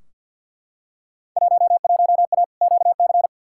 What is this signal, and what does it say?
Morse code, heard as a keyed tone, about 52 wpm, 700 Hz. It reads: 11A QC
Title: Field Day Exchanges Encore - 50wpm